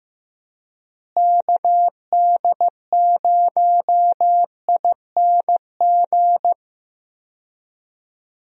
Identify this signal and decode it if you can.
Morse code, heard as a keyed tone, about 15 wpm, 700 Hz. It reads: KD0ING
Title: USA Call Signs - 6 Characters - Repeated 2x - 15wpm